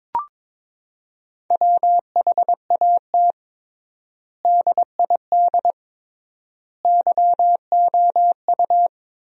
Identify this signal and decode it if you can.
Morse code, heard as a keyed tone, about 22 wpm, 700 Hz. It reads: WHAT DID YOU
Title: 3x Word-Spacing - Sentences from Top 200 Words - 22wpm